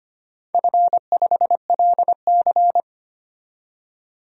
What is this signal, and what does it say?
Morse code, heard as a keyed tone, about 25 wpm, 700 Hz. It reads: F5LC
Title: DXCC-100 Call Signs - 4 Characters - 25wpm